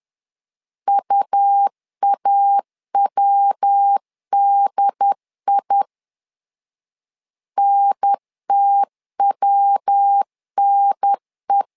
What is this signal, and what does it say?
Morse code, heard as a keyed tone, about 10 wpm, 800 Hz. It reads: UAWDI NTWNE
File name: LEZIONI / 3 MWRO / 3 MW 10wpm 1959